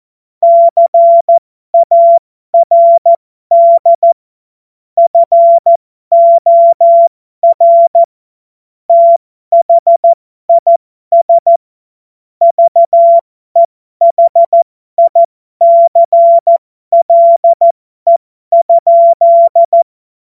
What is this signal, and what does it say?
Morse code, heard as a keyed tone, about 14 wpm, spaced as for 12 wpm, 680 Hz. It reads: CARD FOR THIS VEHICLE?
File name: Morsekurs / Witze